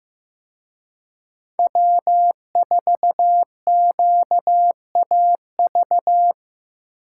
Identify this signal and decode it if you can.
Morse code, heard as a keyed tone, about 18 wpm, 700 Hz. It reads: W4QAV
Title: Speed Racing - USA 1x3 Call Signs 15wpm